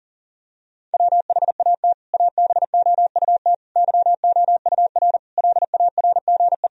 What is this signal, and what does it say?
Morse code, heard as a keyed tone, about 37 wpm, 700 Hz. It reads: WHAT ABOUT YOUR LARGE
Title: Sentences from Top 1000 Words - Review Part 1 - 40wpm